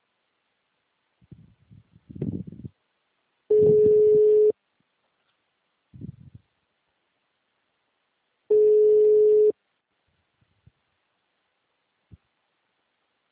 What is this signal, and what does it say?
Ringback tone (Europe, 425 Hz).